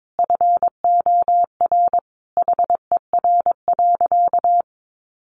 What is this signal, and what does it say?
Morse code, heard as a keyed tone, about 22 wpm, 700 Hz. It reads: FOR HER.
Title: Sentences from Top 600 Words 22wpm